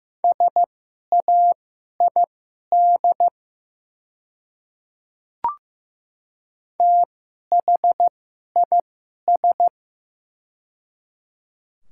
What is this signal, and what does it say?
Morse code, heard as a keyed tone, about 15 wpm, spaced as for 11 wpm, 700 Hz. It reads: SAID THIS
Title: 2x Character Spacing - Top 200 Words - 15wpm